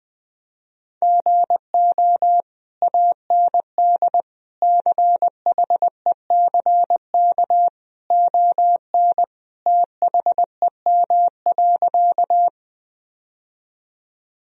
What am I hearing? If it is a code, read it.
Morse code, heard as a keyed tone, about 20 wpm, 700 Hz. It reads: GO AND CHECK ON THEM.
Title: Sentences from Top 1000 Words - Review Part 3 - 20wpm